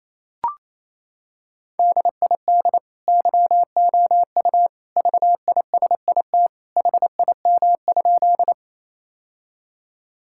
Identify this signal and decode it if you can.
Morse code, heard as a keyed tone, about 28 wpm, 700 Hz. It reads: DID YOU VISIT HIM?
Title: Sentences from Top 1000 Words - Review Part 6 - 28wpm